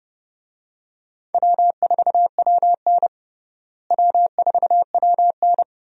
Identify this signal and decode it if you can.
Morse code, heard as a keyed tone, about 30 wpm, 700 Hz. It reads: W4WN W4WN
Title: USA Call Signs - 4 Characters - Repeated 2x - 30wpm